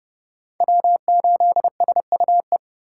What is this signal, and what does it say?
Morse code, heard as a keyed tone, about 31 wpm, 700 Hz. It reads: W8SUE